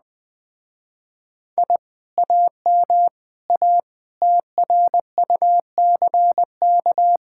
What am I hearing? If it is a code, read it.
Morse code, heard as a keyed tone, about 20 wpm, 700 Hz. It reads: I AM A TRUCK
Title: Speed-Racing Common QSO Phrases Encore - 17wpm